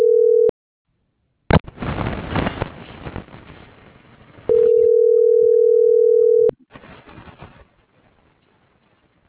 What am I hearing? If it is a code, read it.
Ringback tone (North America), 440 + 480 Hz.